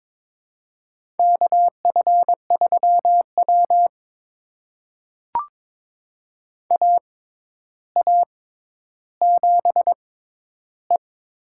Morse code, heard as a keyed tone, about 22 wpm, spaced as for 11 wpm, 700 Hz. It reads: KF3W AA7E
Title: USA Call Signs - 4 Characters - 6x Character Spacing - 22wpm